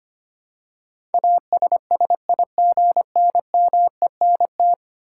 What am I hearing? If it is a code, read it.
Morse code, heard as a keyed tone, about 25 wpm, 700 Hz. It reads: ASSIGNMENT
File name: 10 Letter Words - First 3 Letters Given - 25wpm